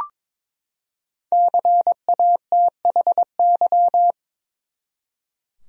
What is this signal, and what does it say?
Morse code, heard as a keyed tone, about 22 wpm, 700 Hz. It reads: CATHY
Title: USA Common Names - 22wpm